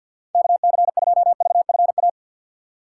Morse code, heard as a keyed tone, about 53 wpm, 700 Hz. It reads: KC2VFA
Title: USA Call Signs - 6 Characters - 4x Character Spacing - 50wpm